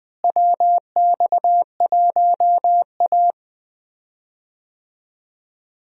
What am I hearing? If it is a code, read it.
Morse code, heard as a keyed tone, about 21 wpm, 700 Hz. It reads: WX1A
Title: USA Call Signs - 4 Characters - Repeated 2x - 20wpm